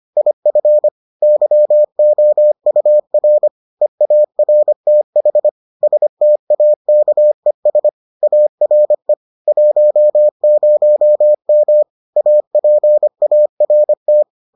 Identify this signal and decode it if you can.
Morse code, heard as a keyed tone, about 25 wpm, 600 Hz. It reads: IF YOUR EARTH STAKES ARE 10M APART